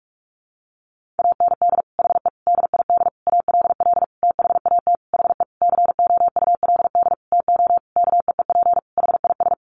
Morse code, heard as a keyed tone, about 44 wpm, 700 Hz. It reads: AND HE DID ALL THAT HE COULD TO KEEP HIS
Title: Sentences from Top 1500 Words 45wpm